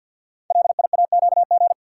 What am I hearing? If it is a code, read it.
Morse code, heard as a keyed tone, about 52 wpm, 700 Hz. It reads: LIAQG
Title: Code Groups - 5 Letters - 50wpm